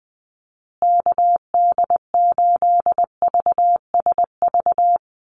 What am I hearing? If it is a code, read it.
Morse code, heard as a keyed tone, about 20 wpm, 700 Hz. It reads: KD8VSV
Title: USA Call Signs Repeated 3x 20wpm